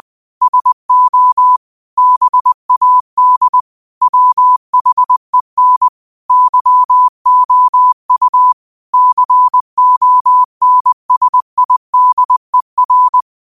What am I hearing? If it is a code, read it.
Morse code, heard as a keyed tone, about 20 wpm, 1000 Hz. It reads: SO BAD WHEN YOU CONSIDER
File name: qotdfiles / 20wpm